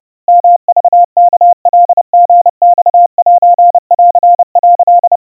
Morse code (Japanese Wabun code), heard as a keyed tone, about 30 wpm, 700 Hz. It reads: ヨクワカリマセン」
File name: 110 cpm 700 Hz Crispy